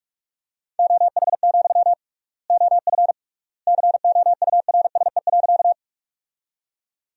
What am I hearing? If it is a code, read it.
Morse code, heard as a keyed tone, about 46 wpm, 700 Hz. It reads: OH, OF COURSE.
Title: 3x Word-Spacing - Sentences from Top 500 Words - 45wpm